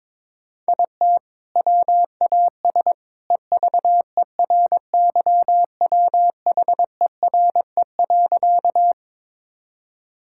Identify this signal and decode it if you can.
Morse code, heard as a keyed tone, about 22 wpm, 700 Hz. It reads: IT WAS EVERYWHERE.
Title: Sentences from Top 1600 Words 22wpm